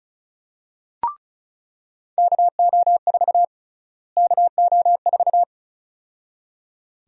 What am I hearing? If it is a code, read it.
Morse code, heard as a keyed tone, about 36 wpm, 700 Hz. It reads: KO4 KO4
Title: USA Call Sign Prefixes - Repeated 2x - 35wpm